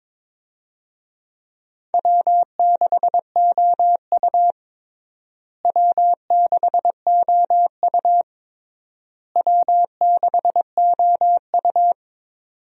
Morse code, heard as a keyed tone, about 21 wpm, 700 Hz. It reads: W6OU W6OU W6OU
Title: USA Call Signs - 4 Characters - Repeated 3x - 22wpm